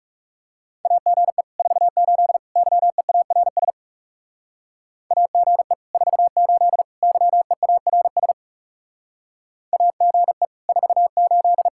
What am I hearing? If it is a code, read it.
Morse code, heard as a keyed tone, about 36 wpm, 700 Hz. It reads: AGE 48 YEARS AGE 48 YEARS AGE 48
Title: Speed-Racing Common QSO Phrases Encore - 30wpm